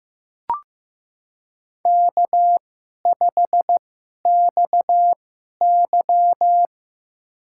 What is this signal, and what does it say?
Morse code, heard as a keyed tone, about 15 wpm, spaced as for 11 wpm, 700 Hz. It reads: K5XY
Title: USA Call Signs - 4 Characters - 2x Character Spacing - 15wpm